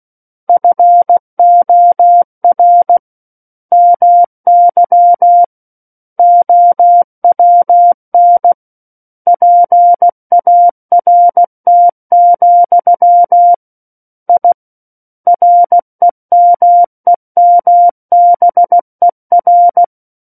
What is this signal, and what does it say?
Morse code, heard as a keyed tone, about 16 wpm, 700 Hz. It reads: FOR MY OWN PART, I REMEMBER
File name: Book One / 07-Chapter 7 - 16 WPM